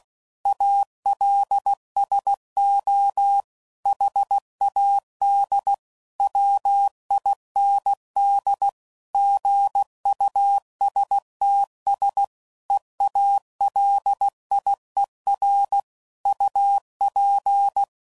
Morse code, heard as a keyed tone, about 16 wpm, 800 Hz. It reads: ALSO HAD WIND GUSTS EALIER UP
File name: CW Code Groups @15WPM / Weather Sets / Weather 5 5